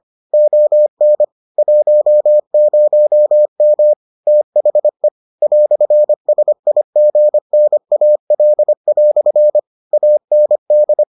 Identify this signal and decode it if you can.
Morse code, heard as a keyed tone, about 25 wpm, 600 Hz. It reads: ON 10M THE "SIGNAL" AND